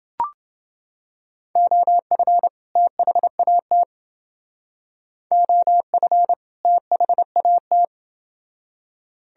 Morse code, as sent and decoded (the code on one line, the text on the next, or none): --- ..-. / - .... .- - / --- ..-. / - .... .- -
OF THAT OF THAT